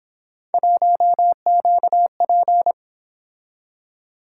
Morse code, heard as a keyed tone, about 28 wpm, 700 Hz. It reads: 1QP